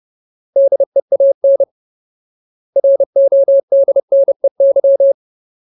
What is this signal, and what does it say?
Morse code, heard as a keyed tone, about 30 wpm, 550 Hz. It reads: DEAN RODNEY
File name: Ultimate Word List / 30WPM Ultimate